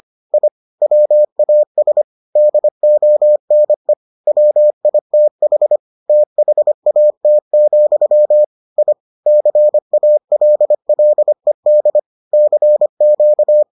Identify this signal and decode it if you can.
Morse code, heard as a keyed tone, about 25 wpm, 600 Hz. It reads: I WAS DONE WITH THAT, I CALLED CQ